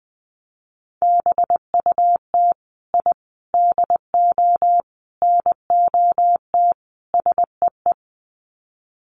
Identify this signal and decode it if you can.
Morse code, heard as a keyed tone, about 20 wpm, 700 Hz. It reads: BUT I DO NOT SEE
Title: Sets of 5 Words Encore 20wpm